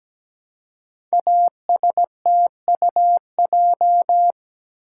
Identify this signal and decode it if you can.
Morse code, heard as a keyed tone, about 17 wpm, 700 Hz. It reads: ASTUJ
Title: Code Groups - 5 Letters - 17wpm